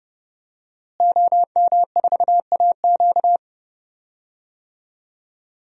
Morse code, heard as a keyed tone, about 30 wpm, 700 Hz. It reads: OM4AQ
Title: DXCC-100 Call Signs - 5 Characters - Repeated 3x - 30wpm